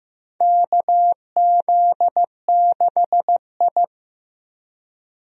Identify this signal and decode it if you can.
Morse code, heard as a keyed tone, about 15 wpm, 700 Hz. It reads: KZ6I
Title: USA Call Signs - 4 Characters - Repeated 3x - 15wpm